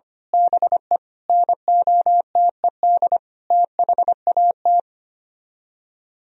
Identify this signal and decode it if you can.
Morse code, heard as a keyed tone, about 25 wpm, 700 Hz. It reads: BE NOTED THAT
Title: Speed-Racing - Sets of 5 Words 25wpm